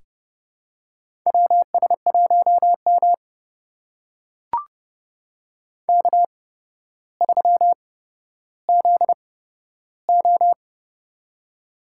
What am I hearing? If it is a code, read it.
Morse code, heard as a keyed tone, about 30 wpm, spaced as for 13 wpm, 700 Hz. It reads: WS1M K3ZO
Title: USA Call Signs - 4 Characters - 8x Character Spacing - 30wpm